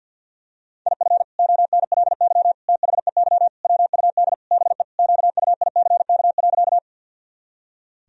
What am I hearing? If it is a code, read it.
Morse code, heard as a keyed tone, about 49 wpm, 700 Hz. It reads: IF ONLY THEY WUD BE QUICK.
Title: Sentences from Top 500 Words and Abbreviations 50wpm